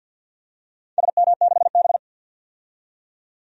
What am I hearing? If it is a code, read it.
Morse code, heard as a keyed tone, about 52 wpm, 700 Hz. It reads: SM6B
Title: DXCC-100 Call Signs - 4 Characters - 50wpm